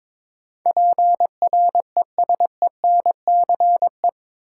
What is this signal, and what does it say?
Morse code, heard as a keyed tone, about 21 wpm, 700 Hz. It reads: PRESENCE
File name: practice / 22wpm / Eight Letter Words - 22wpm